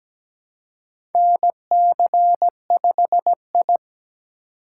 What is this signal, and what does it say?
Morse code, heard as a keyed tone, about 17 wpm, 700 Hz. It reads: NC5I
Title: USA Call Signs - 4 Characters - 4x Character Spacing - 17wpm